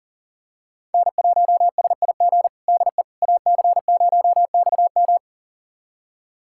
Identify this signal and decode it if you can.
Morse code, heard as a keyed tone, about 37 wpm, 700 Hz. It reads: N1SIG DE AC0XM
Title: USA Callsign DE Callsign - 5 Characters - 40wpm